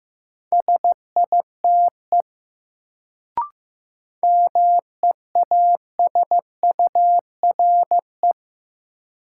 Morse code, heard as a keyed tone, about 15 wpm, 700 Hz. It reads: SITE MEASURE
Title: Sentences from Top 1000 Words - Review Part 1 - New Word Drill - 15wpm